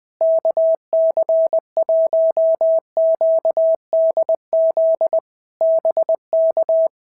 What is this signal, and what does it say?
Morse code, heard as a keyed tone, about 20 wpm, 650 Hz. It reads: KC1QDZ BK
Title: POTA QSO Practice - 20wpm